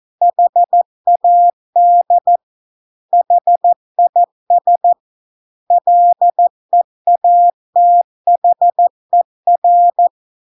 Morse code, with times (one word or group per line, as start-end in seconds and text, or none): 0.2-2.4 HAD
3.1-4.9 HIS
5.7-10.1 LEATHER